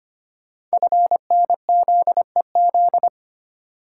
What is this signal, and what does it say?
Morse code, heard as a keyed tone, about 25 wpm, 700 Hz. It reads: FNZEZ
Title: Code Groups - 5 Letters - 25wpm